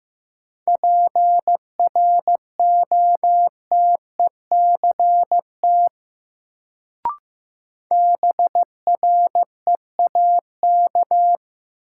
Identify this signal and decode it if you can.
Morse code, heard as a keyed tone, about 15 wpm, 700 Hz. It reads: PROTECT BREAK
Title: Sentences from Top 900 Words - New Word Drill - 15wpm